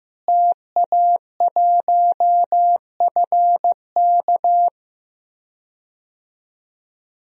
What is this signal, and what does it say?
Morse code, heard as a keyed tone, about 15 wpm, 700 Hz. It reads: TA1FK
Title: DXCC-100 Call Signs - 5 Characters - Repeated 3x - 15wpm